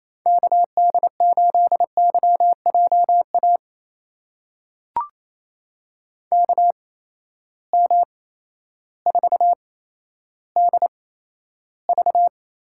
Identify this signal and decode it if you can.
Morse code, heard as a keyed tone, about 28 wpm, spaced as for 13 wpm, 700 Hz. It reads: KD8YJA KM4DV
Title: USA Call Signs - 6 Characters - 8x Character Spacing - 28wpm